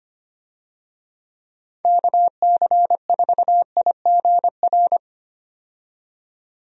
Morse code, heard as a keyed tone, about 25 wpm, 700 Hz. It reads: KC4IGR
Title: USA Call Signs - 6 Characters - 25wpm